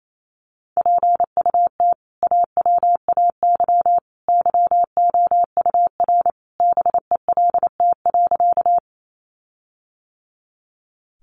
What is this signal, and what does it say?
Morse code, heard as a keyed tone, about 28 wpm, 700 Hz. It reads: PUT AWAY YOUR BELT.
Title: Sentences from Top 1400 Words 28wpm